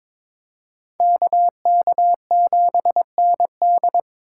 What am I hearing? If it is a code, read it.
Morse code, heard as a keyed tone, about 26 wpm, 700 Hz. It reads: KK7ND
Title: Speed Racing - USA 2x2 Call Signs 22wpm